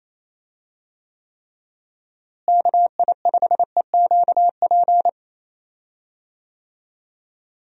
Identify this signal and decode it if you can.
Morse code, heard as a keyed tone, about 28 wpm, 700 Hz. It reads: KI5EQP
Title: USA Call Signs - 6 Characters - Repeated 3x - 28wpm